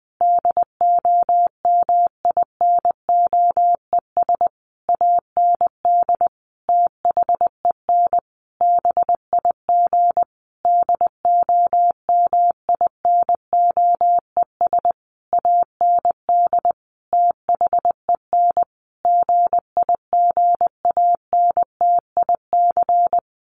Morse code, heard as a keyed tone, about 20 wpm, 700 Hz. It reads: DOMINOES AND THEN BIG DOMINOES AND THEN GIGANTIC
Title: A Sound Of Thunder 20wpm